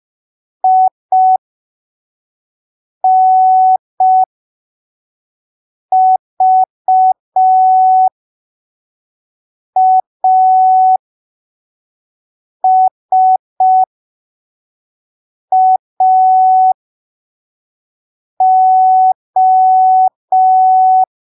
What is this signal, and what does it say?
Morse code, heard as a keyed tone, about 5 wpm, spaced as for 3 wpm, 750 Hz. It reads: INVASAO
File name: arquivos / texto / texto 30 05ppm